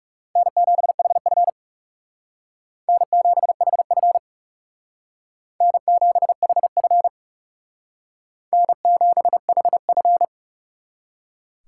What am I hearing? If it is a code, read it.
Morse code, heard as a keyed tone, about 40 wpm, 700 Hz. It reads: N7HF N7HF N7HF N7HF